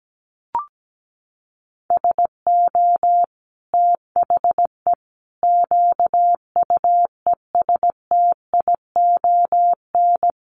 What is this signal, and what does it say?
Morse code, heard as a keyed tone, about 17 wpm, 700 Hz. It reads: SO THE QUESTION
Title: Sentences from Top 400 Words 17wpm